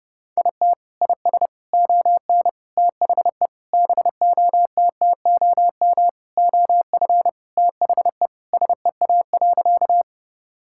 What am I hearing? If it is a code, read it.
Morse code, heard as a keyed tone, about 30 wpm, 700 Hz. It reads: IT IS ON THE BOTTOM OF THE SEA.
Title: Sentences from Top 1000 Words - Review Part 6 - 30wpm